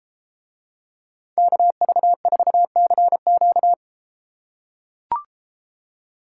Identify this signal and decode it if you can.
Morse code, heard as a keyed tone, about 32 wpm, 700 Hz. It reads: KV4CQ